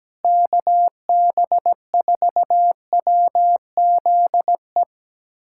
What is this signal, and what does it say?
Morse code, heard as a keyed tone, about 17 wpm, 700 Hz. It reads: KB4WZE
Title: USA Callsign DE Callsign 17wpm